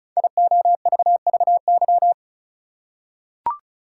Morse code, heard as a keyed tone, about 36 wpm, 700 Hz. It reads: IOVVY
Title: Code Groups - 5 Letter:Numbers - 35wpm